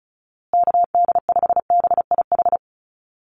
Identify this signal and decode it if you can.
Morse code, heard as a keyed tone, about 34 wpm, 700 Hz. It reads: KD5BIH
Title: USA Call Signs Repeated 3x 35wpm